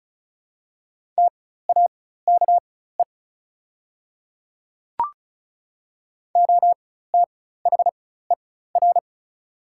Morse code, heard as a keyed tone, about 36 wpm, spaced as for 16 wpm, 700 Hz. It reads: TAKE OTHER
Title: 4x Character Spacing - Top 300 Words - 35wpm